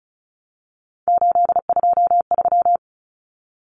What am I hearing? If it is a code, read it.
Morse code, heard as a keyed tone, about 34 wpm, 700 Hz. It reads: 823
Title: Common Cut Numbers - 3-Digit 35wpm